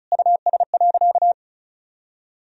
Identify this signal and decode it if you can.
Morse code, heard as a keyed tone, about 34 wpm, 700 Hz. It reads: US.
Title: Sentences from Top 1000 Words - Review Part 4 - 35wpm